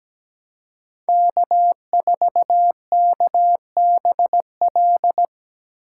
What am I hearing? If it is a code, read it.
Morse code, heard as a keyed tone, about 17 wpm, 700 Hz. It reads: K4KBL
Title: K1USN SST QSO Practice - 17wpm